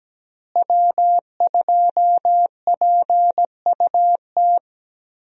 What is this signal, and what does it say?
Morse code, heard as a keyed tone, about 17 wpm, 700 Hz. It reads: W2PUT
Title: USA Call Signs - 5 Characters - Repeated 2x - 17wpm